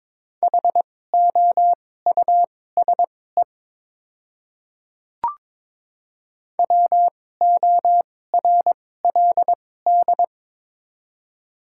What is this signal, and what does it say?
Morse code, heard as a keyed tone, about 22 wpm, spaced as for 16 wpm, 700 Hz. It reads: HOUSE WORLD
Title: 2x Character Spacing - Top 200 Words - 22wpm